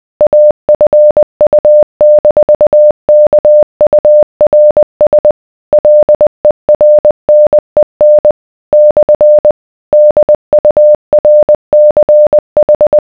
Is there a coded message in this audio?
Yes: Morse code from 0.2 to 13.0 s, heard as a keyed tone, about 20 wpm, 600 Hz. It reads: AFU-KURS LERNEN / DURCH